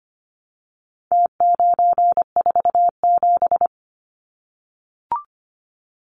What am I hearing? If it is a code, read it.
Morse code, heard as a keyed tone, about 25 wpm, 700 Hz. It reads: T947
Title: Numbers - Contest Style - 4-Digit 25wpm